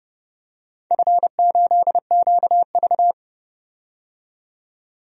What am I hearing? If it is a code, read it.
Morse code, heard as a keyed tone, about 30 wpm, 700 Hz. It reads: F8QV
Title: DXCC-100 Call Signs - 4 Characters - 30wpm